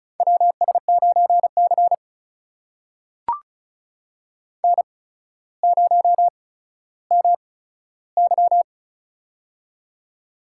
Morse code, heard as a keyed tone, about 36 wpm, spaced as for 15 wpm, 700 Hz. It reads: WS9C N0MY
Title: USA Call Signs - 4 Characters - 8x Character Spacing - 35wpm